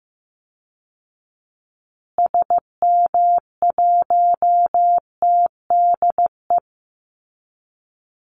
Morse code, heard as a keyed tone, about 15 wpm, 700 Hz. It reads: SM1TDE
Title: Call Signs Repeated 3x - 15wpm